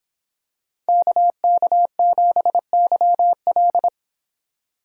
Morse code, heard as a keyed tone, about 25 wpm, 700 Hz. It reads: KK7YL